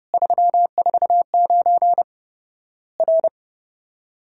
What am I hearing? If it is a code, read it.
Morse code, heard as a keyed tone, about 30 wpm, 700 Hz. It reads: 349 R
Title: Worked All Europe DX Contest - QTC Exchange - 30wpm